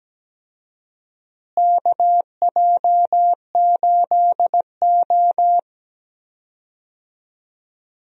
Morse code, heard as a keyed tone, about 17 wpm, 700 Hz. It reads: KJ8O
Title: USA Call Signs - 4 Characters - Repeated 2x - 17wpm